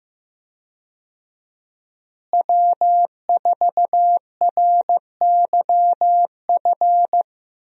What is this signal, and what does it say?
Morse code, heard as a keyed tone, about 15 wpm, 700 Hz. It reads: W4RYF
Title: USA Call Signs - 5 Characters - Repeated 2x - 15wpm